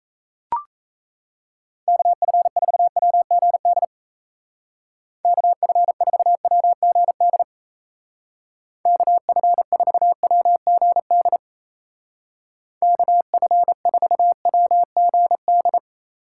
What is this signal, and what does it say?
Morse code, heard as a keyed tone, about 34 wpm, 700 Hz. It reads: KF4WGD KF4WGD KF4WGD KF4WGD
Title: Speed Racing - USA 2x3 Call Signs 28wpm